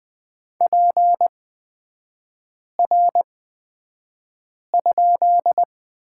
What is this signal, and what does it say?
Morse code, heard as a keyed tone, about 20 wpm, spaced as for 5 wpm, 700 Hz. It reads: PR?
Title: 224 Lesson 36e - BT - Three Characters 20wpm